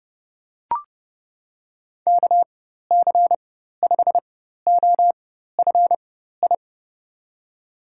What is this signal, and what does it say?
Morse code, heard as a keyed tone, about 30 wpm, spaced as for 14 wpm, 700 Hz. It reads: KC5OFI
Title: USA Call Signs - 6 Characters - 4x Character Spacing - 30wpm